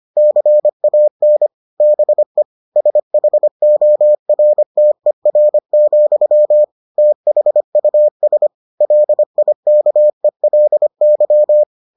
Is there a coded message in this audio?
Yes: Morse code from 0.2 to 11.6 s, heard as a keyed tone, about 25 wpm, 600 Hz. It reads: CAN BE SHORTER, THUS LIKELY